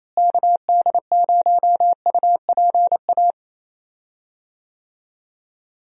Morse code, heard as a keyed tone, about 27 wpm, 700 Hz. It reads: KD0UPA
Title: USA Call Signs - 6 Characters - Repeated 2x - 28wpm